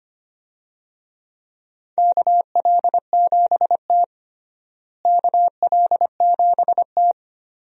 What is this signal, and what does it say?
Morse code, heard as a keyed tone, about 25 wpm, 700 Hz. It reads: KL7T KL7T
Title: USA Call Signs - 4 Characters - Repeated 2x - 25wpm